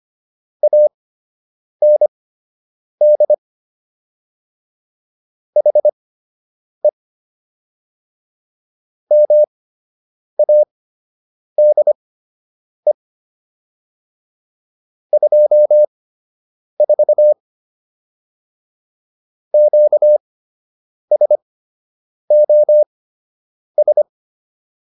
Morse code, heard as a keyed tone, about 25 wpm, spaced as for 8 wpm, 600 Hz. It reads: AND HE MADE 24 QSOS